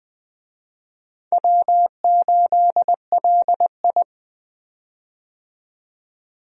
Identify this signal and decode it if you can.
Morse code, heard as a keyed tone, about 20 wpm, 700 Hz. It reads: W8LI
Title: USA Call Signs - 4 Characters - 20wpm